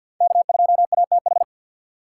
Morse code, heard as a keyed tone, about 62 wpm, 700 Hz. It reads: K2ATH
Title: Speed Racing - USA 1x3 Call Signs 50wpm